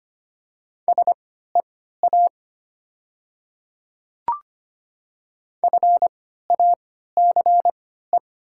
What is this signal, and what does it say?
Morse code, heard as a keyed tone, about 25 wpm, spaced as for 14 wpm, 700 Hz. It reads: SEA FACE
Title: 3x Character Spacing - Top 300 Words - 25wpm